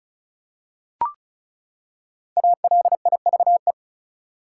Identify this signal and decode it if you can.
Morse code, heard as a keyed tone, about 36 wpm, 700 Hz. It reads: ALIVE
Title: Five Letter Words - 35wpm